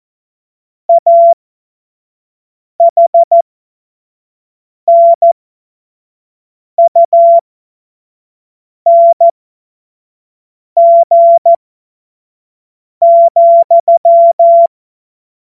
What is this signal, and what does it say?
Morse code, heard as a keyed tone, about 14 wpm, spaced as for 5 wpm, 680 Hz. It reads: AHNUNG,
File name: Morsekurs / Witze / Witz 31. August 2021, 25 F70